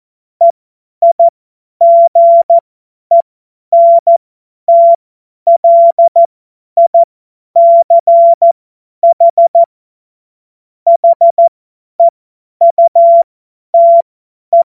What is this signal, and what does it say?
Morse code, heard as a keyed tone, about 14 wpm, spaced as for 10 wpm, 680 Hz. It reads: EIGENTLICH HEUTE
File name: Morsekurs / Witze / Witz 25. April 2023 50 F70